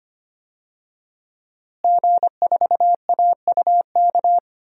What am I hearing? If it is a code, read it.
Morse code, heard as a keyed tone, about 25 wpm, 700 Hz. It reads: G4AUK
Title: DXCC-100 Call Signs - 5 Characters - Repeated 3x - 25wpm